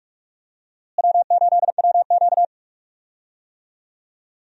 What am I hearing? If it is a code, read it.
Morse code, heard as a keyed tone, about 44 wpm, 700 Hz. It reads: W8WQ
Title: USA Call Signs - 4 Characters - Repeated 2x - 45wpm